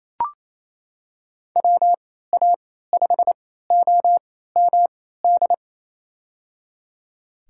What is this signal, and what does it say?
Morse code, heard as a keyed tone, about 28 wpm, spaced as for 16 wpm, 700 Hz. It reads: WA5OMD
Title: USA Call Signs - 6 Characters - 3x Character Spacing - 28wpm